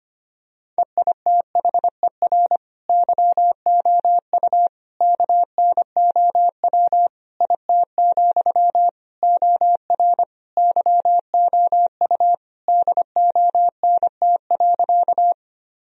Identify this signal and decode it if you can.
Morse code, heard as a keyed tone, about 25 wpm, 700 Hz. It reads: EITHER YOU KNOW IT, OR YOU DONT.
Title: Sentences from Top 1000 Words - Review Part 4 - 25wpm